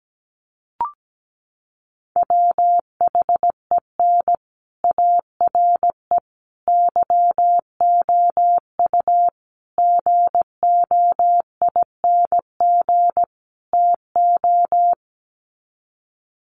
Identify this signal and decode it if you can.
Morse code, heard as a keyed tone, about 17 wpm, 700 Hz. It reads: WHEN ARE YOU GOING TO
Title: Sets of 5 Words Encore2 17wpm